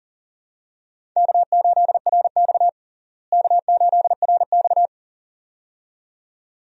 Morse code, heard as a keyed tone, about 37 wpm, 700 Hz. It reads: K8RX K8RX
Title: USA Call Signs - 4 Characters - Repeated 2x - 40wpm